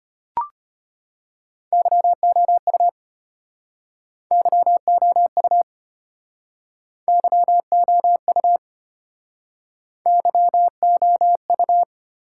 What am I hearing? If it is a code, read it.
Morse code, heard as a keyed tone, about 32 wpm, 700 Hz. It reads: YOU YOU YOU YOU